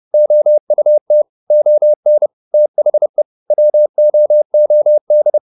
Morse code, heard as a keyed tone, about 30 wpm, 600 Hz. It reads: OUT ON THE WOOD
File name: the old man and the sea / the old man and the sea 1 30wpm